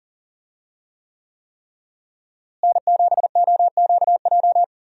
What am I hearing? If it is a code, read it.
Morse code, heard as a keyed tone, about 37 wpm, 700 Hz. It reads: N7OQJ
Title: USA Call Signs - 5 Characters - Repeated 3x - 40wpm